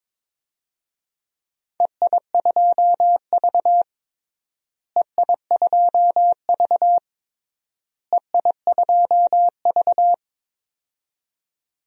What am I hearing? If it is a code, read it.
Morse code, heard as a keyed tone, about 22 wpm, 700 Hz. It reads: EI2V EI2V EI2V
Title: DXCC-100 Call Signs - 4 Characters - Repeated 3x - 22wpm